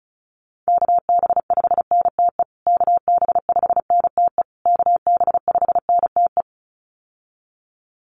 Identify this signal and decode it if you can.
Morse code, heard as a keyed tone, about 34 wpm, 700 Hz. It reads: KB5NTE KB5NTE KB5NTE
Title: USA Call Signs Repeated 3x 35wpm